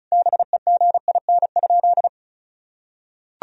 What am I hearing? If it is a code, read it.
Morse code, heard as a keyed tone, about 34 wpm, 700 Hz. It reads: BEGIN?
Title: Sentences from Top 1000 Words - Review Part 1 - 35wpm